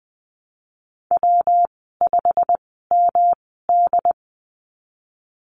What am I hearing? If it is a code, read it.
Morse code, heard as a keyed tone, about 20 wpm, spaced as for 15 wpm, 700 Hz. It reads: W5MD
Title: 128 Lesson 21g - Number 6 - Call Signs 20wpm